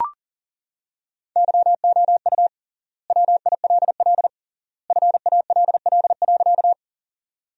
Morse code, heard as a keyed tone, about 40 wpm, 700 Hz. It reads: YOU WILL FALL.
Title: 3x Word-Spacing - Sentences from Top 500 Words - 40wpm